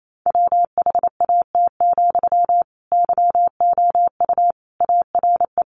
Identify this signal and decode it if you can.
Morse code, heard as a keyed tone, about 28 wpm, 700 Hz. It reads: WHAT, YOU ARE
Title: Sentences from Top 1300 Words 28wpm